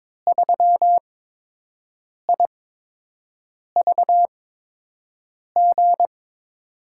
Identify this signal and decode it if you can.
Morse code, heard as a keyed tone, about 22 wpm, spaced as for 6 wpm, 700 Hz. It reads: 3IVG
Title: USA Call Signs - 5 Characters - 8x Character Spacing - 22wpm